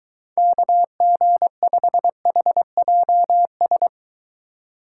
Morse code, heard as a keyed tone, about 23 wpm, 700 Hz. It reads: KG5HJS